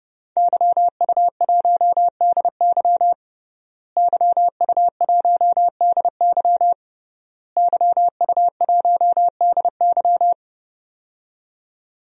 Morse code, heard as a keyed tone, about 30 wpm, 700 Hz. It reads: YU1DY YU1DY YU1DY
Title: DXCC-100 Call Signs - 5 Characters - Repeated 3x - 30wpm